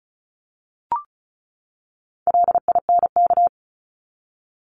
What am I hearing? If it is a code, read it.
Morse code, heard as a keyed tone, about 34 wpm, 700 Hz. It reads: LINK
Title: Top 751-1000 Words 35wpm